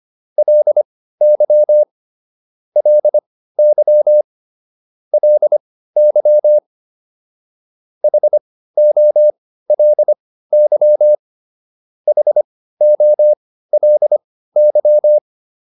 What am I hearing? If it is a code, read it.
Morse code, heard as a keyed tone, about 25 wpm, spaced as for 15 wpm, 600 Hz. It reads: LY LY LY HOLY HOLY